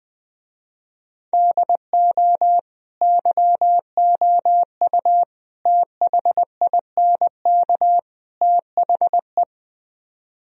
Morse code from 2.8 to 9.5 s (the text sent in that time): YOU THINK THE